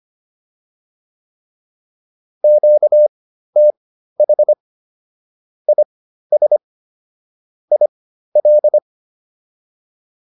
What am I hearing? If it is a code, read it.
Morse code, heard as a keyed tone, about 25 wpm, spaced as for 13 wpm, 600 Hz. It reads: QTH IS IL